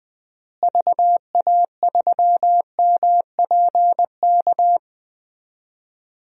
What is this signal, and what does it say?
Morse code, heard as a keyed tone, about 20 wpm, 700 Hz. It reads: VA3MPK
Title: DXCC-100 Call Signs - 6 Characters - Repeated 2x - 20wpm